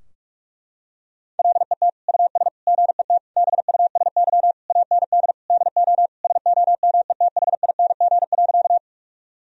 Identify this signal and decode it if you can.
Morse code, heard as a keyed tone, about 42 wpm, 700 Hz. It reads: LET US GET BUSY AND DO SOMETHING.
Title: Sentences from Top 1000 Words - Review Part 3 - 45wpm